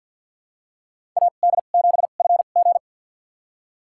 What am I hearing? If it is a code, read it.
Morse code, heard as a keyed tone, about 53 wpm, spaced as for 37 wpm, 700 Hz. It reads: AD7FG
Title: USA Call Signs - 5 Characters - 2x Character Spacing - 50wpm